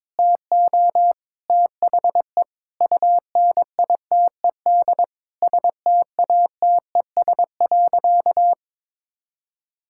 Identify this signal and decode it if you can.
Morse code, heard as a keyed tone, about 22 wpm, 700 Hz. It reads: TO THE UNITED STATES.
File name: practice / 22wpm / Sentences with 2-Word Phrases - 22wpm